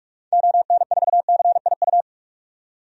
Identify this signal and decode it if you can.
Morse code, heard as a keyed tone, about 46 wpm, 700 Hz. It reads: ON4CIU
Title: DXCC-100 Call Signs - 6 Characters - 45wpm